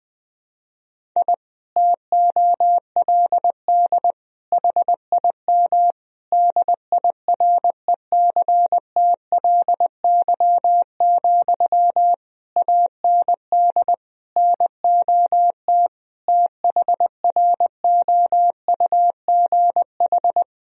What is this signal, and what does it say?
Morse code, heard as a keyed tone, about 20 wpm, 700 Hz. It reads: I TOLD HIM DIRECTLY, AND NOT THROUGH
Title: Sentences from Top 1000 Words - Review Part 1 - 20wpm